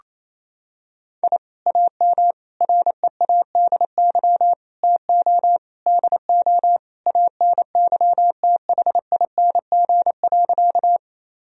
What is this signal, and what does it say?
Morse code, heard as a keyed tone, about 27 wpm, 700 Hz. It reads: I AM READY TO DO ANYTHING.
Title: Sentences from Top 1000 Words - Review Part 2 - 28wpm